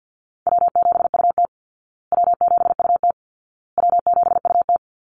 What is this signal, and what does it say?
Morse code, heard as a keyed tone, about 50 wpm, 700 Hz. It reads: W7UT W7UT W7UT
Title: Call Signs Repeated 2x - 50wpm